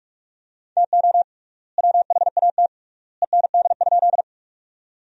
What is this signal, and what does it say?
Morse code, heard as a keyed tone, about 46 wpm, 700 Hz. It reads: TO WHAT END?
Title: 3x Word-Spacing - Sentences from Top 200 Words - 45wpm